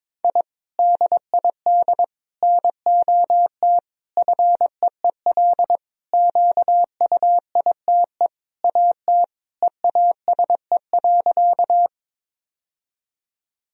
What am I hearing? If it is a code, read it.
Morse code, heard as a keyed tone, about 22 wpm, 700 Hz. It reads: I DID NOT FEEL QUITE AT EASE.
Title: Sentences from Top 1000 Words - Review Part 1 - 22wpm